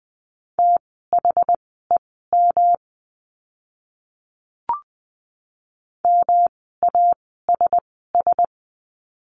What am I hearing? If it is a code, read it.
Morse code, heard as a keyed tone, about 20 wpm, spaced as for 14 wpm, 700 Hz. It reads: THEM MASS
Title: 108 Lesson 18f - Letter m - Words 20wpm